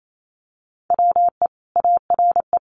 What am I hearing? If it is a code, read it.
Morse code, heard as a keyed tone, about 28 wpm, 700 Hz. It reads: WE ARE
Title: Sentences from Top 1500 Words 28wpm